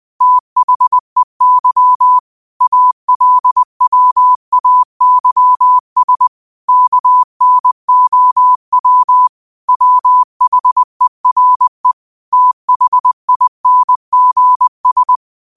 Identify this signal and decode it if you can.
Morse code, heard as a keyed tone, about 20 wpm, 1000 Hz. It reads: THEY ALWAYS KNOW WHERE THINGS
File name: qotdfiles / 20wpm